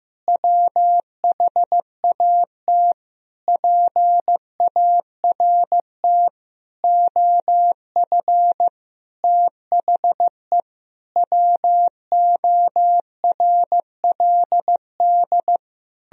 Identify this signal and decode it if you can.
Morse code, heard as a keyed tone, about 16 wpm, 700 Hz. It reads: WHAT PART OF THE WORLD
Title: Sentences from Top 1000 Words - Review Part 3 - 15wpm